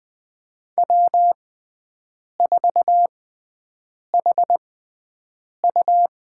Morse code, heard as a keyed tone, about 20 wpm, spaced as for 7 wpm, 700 Hz. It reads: W4HU
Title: USA Call Signs - 6x Character Spacing - 20wpm